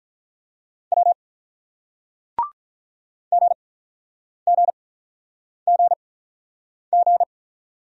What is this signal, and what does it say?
Morse code, heard as a keyed tone, about 49 wpm, spaced as for 9 wpm, 700 Hz. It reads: W GGGG